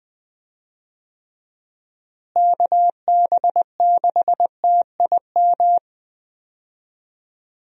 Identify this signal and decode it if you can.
Morse code, heard as a keyed tone, about 20 wpm, 700 Hz. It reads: KB6TIM
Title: USA Call Signs - 6 Characters - Repeated 3x - 20wpm